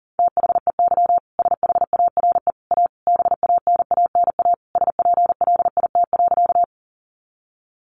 Morse code, heard as a keyed tone, about 37 wpm, 700 Hz. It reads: THEY SHARE A BANANA SPLIT.
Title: Easy Phrases 40wpm